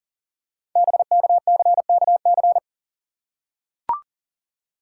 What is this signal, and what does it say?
Morse code, heard as a keyed tone, about 38 wpm, 700 Hz. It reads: BKCKC
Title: Code Groups - 5 Letters - 40wpm